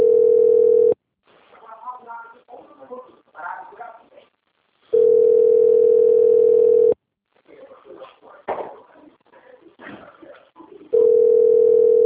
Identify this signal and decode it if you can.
Ringback tone (North America), 440 + 480 Hz, with speech.
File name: out- - - - - .5798